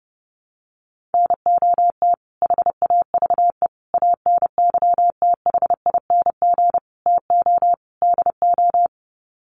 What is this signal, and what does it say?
Morse code, heard as a keyed tone, about 30 wpm, 700 Hz. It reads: NOT HAVE ANYTHING TO DO
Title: Sets of 5 Words Encore 30wpm